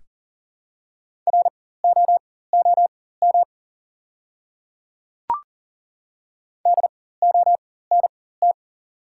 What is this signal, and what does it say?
Morse code, heard as a keyed tone, about 42 wpm, spaced as for 19 wpm, 690 Hz. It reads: ROOM DONT
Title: 4x Character Spacing - Top 300 Words - 40wpm